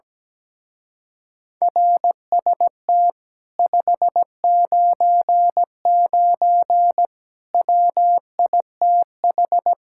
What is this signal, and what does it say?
Morse code, heard as a keyed tone, about 17 wpm, 700 Hz. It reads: RST 599 WITH
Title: Speed-Racing Common QSO Phrases Encore - 17wpm